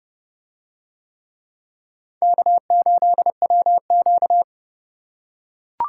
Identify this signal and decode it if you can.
Morse code, heard as a keyed tone, about 28 wpm, 700 Hz. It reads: K8WQ